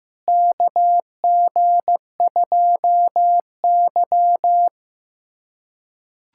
Morse code, heard as a keyed tone, about 15 wpm, 700 Hz. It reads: KG2Y
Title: USA Callsign DE Callsign - 4 Characters - 15wpm